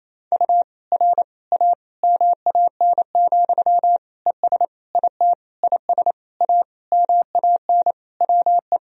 Morse code, heard as a keyed tone, about 28 wpm, 700 Hz. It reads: U R A MAN, ES IT IS A MAN WE
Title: Sentences from Top 200 Words and Abbreviations 28wpm